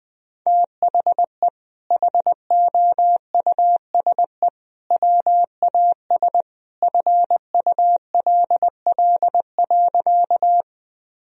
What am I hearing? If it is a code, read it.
Morse code, heard as a keyed tone, about 21 wpm, 700 Hz. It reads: THE HOUSE WAS FULL.
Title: Sentences from Top 1000 Words - Review Part 3 - 20wpm